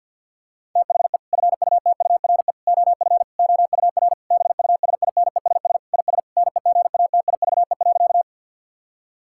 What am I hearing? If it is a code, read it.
Morse code, heard as a keyed tone, about 50 wpm, 700 Hz. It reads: THE FUTURE OF OUR BUSINESS IS NEGATIVE.
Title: Sentences from Top 1000 Words - Review Part 6 - 50wpm